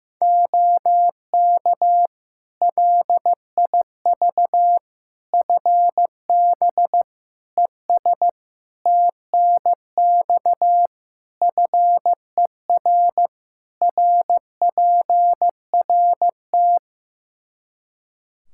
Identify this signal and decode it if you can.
Morse code, heard as a keyed tone, about 15 wpm, 700 Hz. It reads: OK LIV FB ES TNX FER RPRT
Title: Practice QSO - Exchange 2 - Info - 15wpm